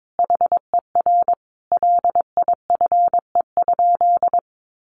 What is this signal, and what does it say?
Morse code, heard as a keyed tone, about 22 wpm, 700 Hz. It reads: HER LIFE?
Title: Sentences from Top 1500 Words 22wpm